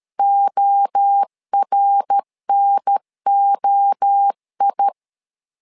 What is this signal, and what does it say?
Morse code, heard as a keyed tone, about 13 wpm, 800 Hz. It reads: ORNOI
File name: LEZIONI / 3 MWRO / 3 MWRO 12wpm 1730